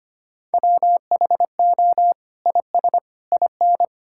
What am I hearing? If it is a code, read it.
Morse code, heard as a keyed tone, about 25 wpm, 700 Hz. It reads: WHO IS IN
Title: Sentences from Top 1000 Words - Review Part 6 - 25wpm